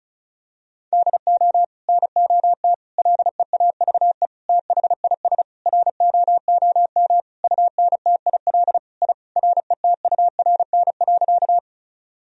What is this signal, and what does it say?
Morse code, heard as a keyed tone, about 34 wpm, 700 Hz. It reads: DO NOT LEAVE THIS ROOM UNTIL I RETURN.
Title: Sentences from Top 1000 Words - Review Part 1 - 35wpm